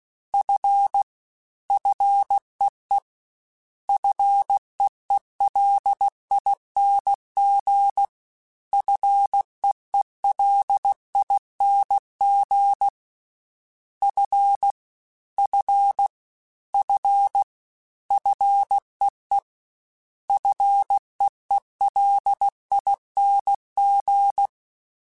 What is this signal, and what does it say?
Morse code, heard as a keyed tone, about 16 wpm, 800 Hz. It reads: F FEE FEELING FEELING F F F FEE FEELING
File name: Prefixes 3x2 Twice @15